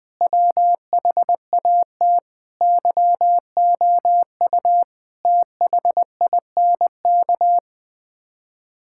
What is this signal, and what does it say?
Morse code, heard as a keyed tone, about 20 wpm, 700 Hz. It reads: WHAT YOU THINK
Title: Sets of 6 Words - 20wpm